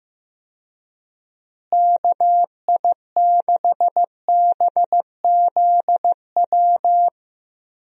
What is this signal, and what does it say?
Morse code, heard as a keyed tone, about 14 wpm, 700 Hz. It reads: KI6BZW
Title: USA Call Signs - 6 Characters - Repeated 3x - 15wpm